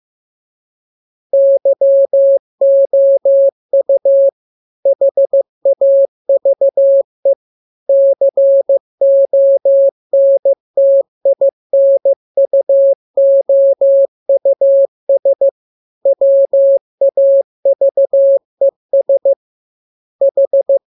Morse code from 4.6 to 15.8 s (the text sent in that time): HAVE CONTINUOUS